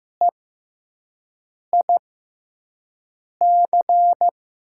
Morse code, heard as a keyed tone, about 16 wpm, spaced as for 5 wpm, 700 Hz. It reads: EIC
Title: USA Call Signs - 5 Characters - 6x Character Spacing - 15wpm